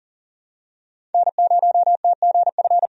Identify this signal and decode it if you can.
Morse code, heard as a keyed tone, about 40 wpm, 700 Hz. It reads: N0TGF
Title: USA Call Signs - 5 Characters - 40wpm